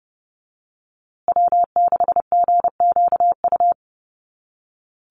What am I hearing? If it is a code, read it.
Morse code, heard as a keyed tone, about 30 wpm, 700 Hz. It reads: W6GQU
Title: USA Call Signs Repeated 2x 30wpm